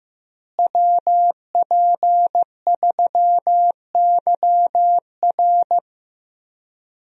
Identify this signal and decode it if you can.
Morse code, heard as a keyed tone, about 15 wpm, 700 Hz. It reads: WP3YR
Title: USA Call Signs - 5 Characters - Repeated 2x - 15wpm